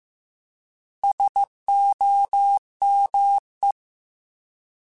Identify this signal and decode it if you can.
Morse code, heard as a keyed tone, about 15 wpm, 800 Hz. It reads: SOME
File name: Alphabetical / Alphabetical mp3s / Ss @13